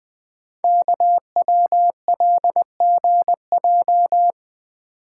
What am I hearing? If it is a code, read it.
Morse code, heard as a keyed tone, about 21 wpm, 700 Hz. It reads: KWLGJ
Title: Code Groups - 5 Letters - 20wpm